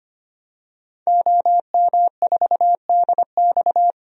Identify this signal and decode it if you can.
Morse code, heard as a keyed tone, about 25 wpm, 700 Hz. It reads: OM4DX
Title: DXCC-100 Call Signs - 5 Characters - 25wpm